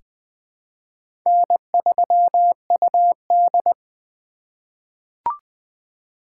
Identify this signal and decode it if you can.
Morse code, heard as a keyed tone, about 21 wpm, 700 Hz. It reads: N3UD
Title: 2023 POTA Activator Call Signs - 20wpm